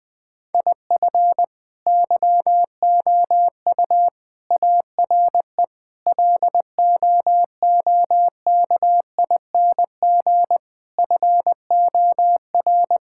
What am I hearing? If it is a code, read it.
Morse code, heard as a keyed tone, about 21 wpm, 700 Hz. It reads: IF YOU ARE LOOKING FOR
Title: Sets of 6 Words - 20wpm